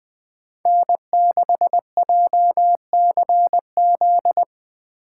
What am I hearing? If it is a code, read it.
Morse code, heard as a keyed tone, about 20 wpm, 700 Hz. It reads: N6JCZ